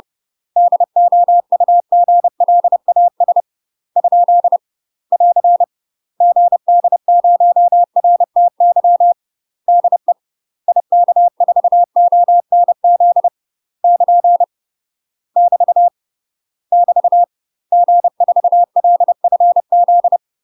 Morse code, heard as a keyed tone, about 30 wpm, 700 Hz. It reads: DOUGLAS ? <AR> GD0RTY DE IK4ONZ <KN> <BT> <BT> G4LFZ
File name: UK MASTER QSOs TRAINING / 30WPM / UK MASTER QSOs 0029